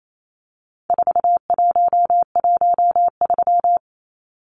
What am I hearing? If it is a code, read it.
Morse code, heard as a keyed tone, about 28 wpm, 700 Hz. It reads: 4113
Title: Numbers - 4-Digit 28wpm